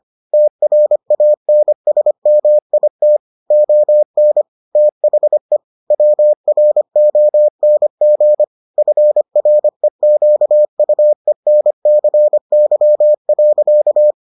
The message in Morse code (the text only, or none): TRANSMIT ON THE WRONG FREQUENCY.